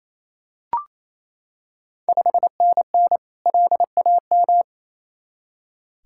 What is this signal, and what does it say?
Morse code, heard as a keyed tone, about 28 wpm, 690 Hz. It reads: 5NN LAM
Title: Ontario QSO Party Exchange 28wpm